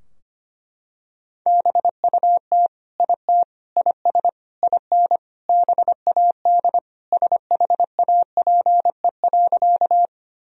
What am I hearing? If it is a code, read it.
Morse code, heard as a keyed tone, about 25 wpm, 700 Hz. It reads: BUT IT IS IN BAD SHAPE.
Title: Sentences from Top 1000 Words - Review Part 3 - 25wpm